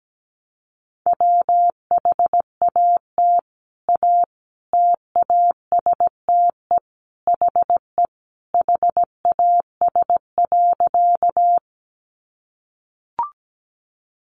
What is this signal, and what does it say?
Morse code, heard as a keyed tone, about 17 wpm, 700 Hz. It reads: WHAT A TASTE HE HAS.
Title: Sentences from Top 1100 Words 17wpm